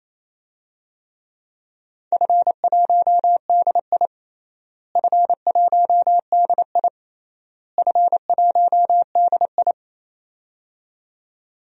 Morse code, heard as a keyed tone, about 28 wpm, 700 Hz. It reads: F1DI F1DI F1DI
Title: DXCC-100 Call Signs - 4 Characters - Repeated 3x - 28wpm